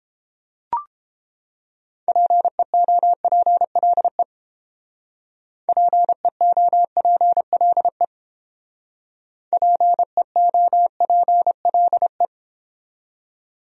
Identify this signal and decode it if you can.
Morse code, heard as a keyed tone, about 30 wpm, 700 Hz. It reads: PEOPLE PEOPLE PEOPLE